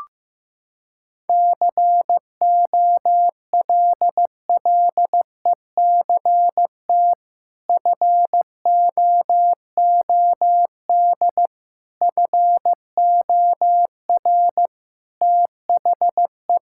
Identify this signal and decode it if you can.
Morse code, heard as a keyed tone, about 15 wpm, 700 Hz. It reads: COLLECT FOOD FOR THE
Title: Sentences from Top 1600 Words 15wpm